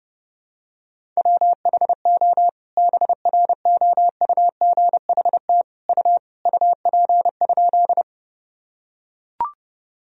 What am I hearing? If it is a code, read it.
Morse code, heard as a keyed tone, about 32 wpm, 700 Hz. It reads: WHO BROUGHT U UP?
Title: Sentences from Top 500 Words and Abbreviations 30wpm